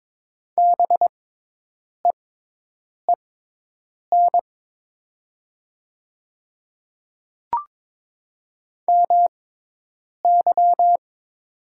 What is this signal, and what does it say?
Morse code, heard as a keyed tone, about 22 wpm, spaced as for 8 wpm, 700 Hz. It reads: BEEN MY
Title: 6x Character Spacing - Top 300 Words - 22wpm